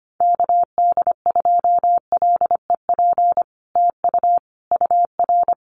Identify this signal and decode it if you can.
Morse code, heard as a keyed tone, about 25 wpm, 700 Hz. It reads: KD2LEP TU UR
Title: Common QSO Phrases Encore 25wpm